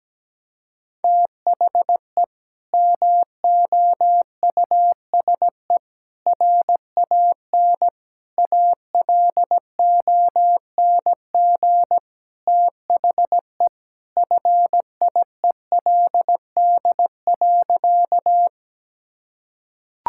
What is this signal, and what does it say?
Morse code, heard as a keyed tone, about 17 wpm, 700 Hz. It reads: THE MOUSE RAN ALONG THE FIELD.
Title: Sentences from Top 1000 Words - Review Part 5 - 17wpm